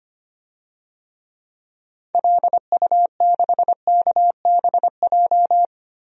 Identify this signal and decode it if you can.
Morse code, heard as a keyed tone, about 25 wpm, 690 Hz. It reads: LU6KBJ
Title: DXCC-100 Call Signs - 6 Characters - Repeated 3x - 25wpm